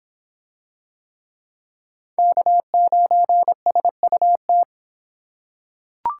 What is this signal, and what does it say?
Morse code, heard as a keyed tone, about 23 wpm, 700 Hz. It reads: K9SUT